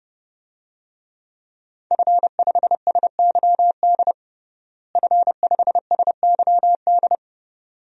Morse code, heard as a keyed tone, about 30 wpm, 700 Hz. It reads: F5SYD F5SYD
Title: DXCC-100 Call Signs - 5 Characters - Repeated 2x - 30wpm